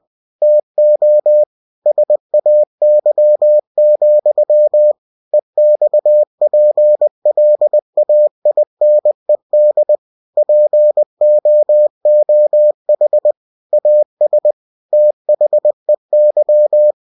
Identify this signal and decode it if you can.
Morse code, heard as a keyed tone, about 20 wpm, 600 Hz. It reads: TO SAY, EXPLAINED POOH AS THEY
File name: cw-books / winnie-the-pooh / winnie-20wpm-page-0099